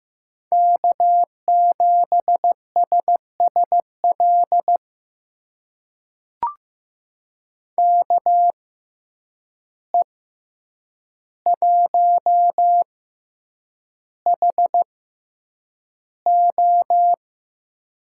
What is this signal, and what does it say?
Morse code, heard as a keyed tone, about 15 wpm, spaced as for 8 wpm, 700 Hz. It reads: K7SSL KE1HO
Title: USA Call Signs - 5 Characters - 6x Character Spacing - 15wpm